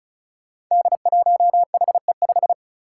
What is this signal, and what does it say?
Morse code, heard as a keyed tone, about 36 wpm, 700 Hz. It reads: D1HE5
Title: Five Challenging Letter-Number Combinations 35wpm